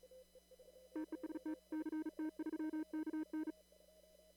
Morse code, heard as a keyed tone, about 38 wpm, 310 Hz. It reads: TEST CT3KN
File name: CW / kt5j wpx cw-op-k5pi